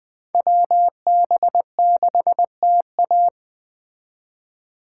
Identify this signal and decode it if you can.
Morse code, heard as a keyed tone, about 21 wpm, 700 Hz. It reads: WB6TA